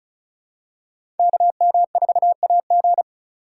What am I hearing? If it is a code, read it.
Morse code, heard as a keyed tone, about 36 wpm, 700 Hz. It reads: KM4AG
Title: USA Call Signs - 2x Character Spacing - 35wpm